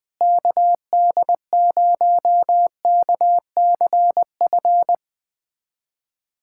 Morse code, heard as a keyed tone, about 21 wpm, 700 Hz. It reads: KD0KCF
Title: USA Callsign DE Callsign - 6 Characters - 20wpm